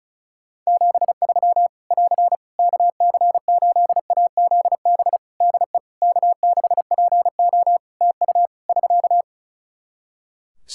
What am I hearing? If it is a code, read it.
Morse code, heard as a keyed tone, about 34 wpm, 700 Hz. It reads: 73 <AR> KC8AZB DE K6PO TU <SK>
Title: Practice QSO - Exchange 4 - Ending - 35wpm